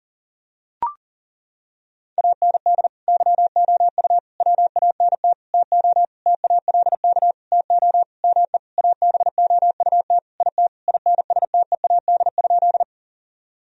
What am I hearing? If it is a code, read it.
Morse code, heard as a keyed tone, about 40 wpm, 700 Hz. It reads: AND YOU WANT TO TALK TO ME ABOUT IT INSTEAD?
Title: Sentences from Top 1000 Words - Review Part 4 - 40wpm